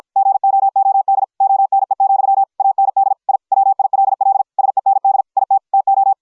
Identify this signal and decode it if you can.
Morse code, heard as a keyed tone, about 53 wpm, 750 Hz. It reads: GOOD ONE, AND I WILL SEND IT TO